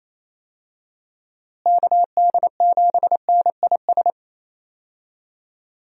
Morse code, heard as a keyed tone, about 28 wpm, 700 Hz. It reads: KD7NIS
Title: USA Call Signs - 6 Characters - Repeated 2x - 28wpm